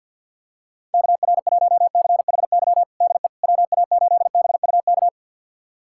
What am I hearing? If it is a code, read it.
Morse code, heard as a keyed tone, about 47 wpm, 700 Hz. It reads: KR1CHY DE WA8BUK
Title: USA Callsign DE Callsign - 6 Characters - 50wpm